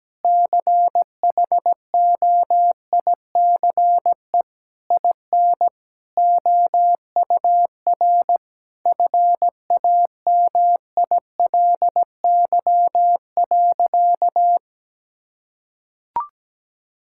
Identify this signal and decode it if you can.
Morse code, heard as a keyed tone, about 17 wpm, 700 Hz. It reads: CHOICE IN OUR FAMILY.
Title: Sentences from Top 1000 Words - Review Part 6 - 17wpm